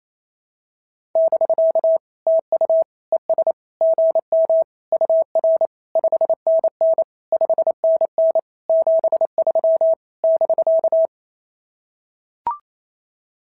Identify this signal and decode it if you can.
Morse code, heard as a keyed tone, about 25 wpm, 650 Hz. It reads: <BK> TU ES GM UR 5NN 5NN 73 <BK>
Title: SOTA QSO Practice 28wpm